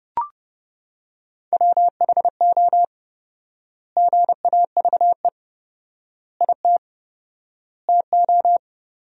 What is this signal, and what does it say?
Morse code, heard as a keyed tone, about 30 wpm, 700 Hz. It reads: WHO GAVE IT TO
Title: 4x Word-Spacing - Sentences from Top 500 Words - 30wpm